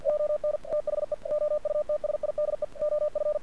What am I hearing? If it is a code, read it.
Morse code, heard as a keyed tone, about 50 wpm, 600 Hz. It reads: ON THE OUTSIDE OF
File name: archived / CRSnet